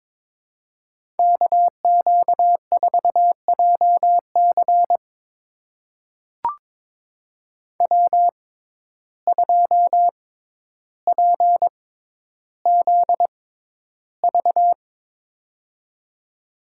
Morse code, heard as a keyed tone, about 22 wpm, spaced as for 11 wpm, 700 Hz. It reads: KQ4JC W2PZV
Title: USA Call Signs - 5 Characters - 6x Character Spacing - 22wpm